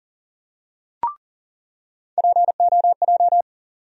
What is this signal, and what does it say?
Morse code, heard as a keyed tone, about 40 wpm, 700 Hz. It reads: POJ
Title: USA Call Sign Postfixes - 3 Letters 40wpm